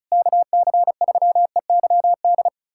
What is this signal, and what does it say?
Morse code, heard as a keyed tone, about 36 wpm, 700 Hz. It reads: KC3EYD
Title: USA Callsign DE Callsign 35wpm